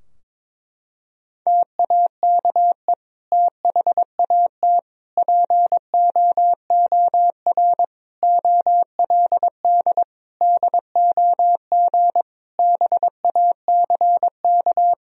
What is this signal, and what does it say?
Morse code, heard as a keyed tone, about 22 wpm, 700 Hz. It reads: TAKE THAT POOR OLD DOG BACK
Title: Sentences from Top 1000 Words - Review Part 2 - 22wpm